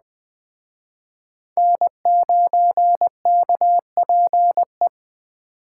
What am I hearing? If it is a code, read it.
Morse code, heard as a keyed tone, about 22 wpm, 700 Hz. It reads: N9KPE